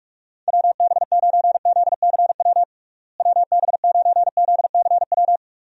Morse code, heard as a keyed tone, about 46 wpm, 700 Hz. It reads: WB9ZCW WB9ZCW
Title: USA Call Signs - 6 Characters - Repeated 2x - 45wpm